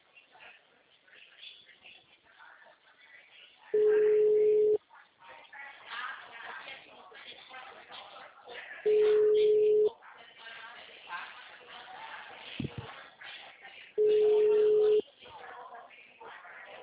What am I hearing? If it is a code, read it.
Ringback tone (Europe, 425 Hz), with speech.